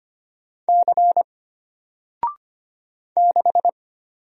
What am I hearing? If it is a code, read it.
Morse code, heard as a keyed tone, about 25 wpm, 700 Hz. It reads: C 6
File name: Single-Letter-Number-25wpm